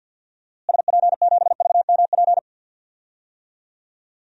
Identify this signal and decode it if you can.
Morse code, heard as a keyed tone, about 51 wpm, 700 Hz. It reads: SP7VMP
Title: DXCC-100 Call Signs - 6 Characters - Repeated 3x - 50wpm